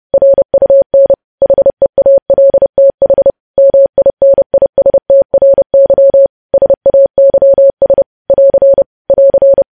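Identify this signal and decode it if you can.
Morse code, heard as a keyed tone, about 30 wpm, 550 Hz. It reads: RUN HEALTH MINISTRY SAYS <AR> <AR>
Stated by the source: Morse news headlines sent at 30 wpm.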